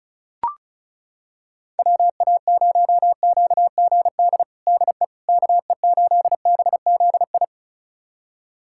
Morse code, heard as a keyed tone, about 36 wpm, 700 Hz. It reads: WA0QGD DE KE8BZI
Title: USA Callsign DE Callsign 35wpm